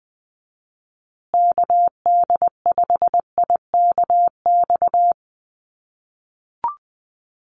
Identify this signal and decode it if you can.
Morse code, heard as a keyed tone, about 19 wpm, 700 Hz. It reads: KD5IKX
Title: USA Call Signs Repeated 2x 20wpm